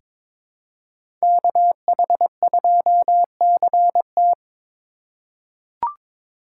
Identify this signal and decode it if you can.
Morse code, heard as a keyed tone, about 22 wpm, 700 Hz. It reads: KH2CT
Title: USA Call Signs - 4x Character Spacing - 22wpm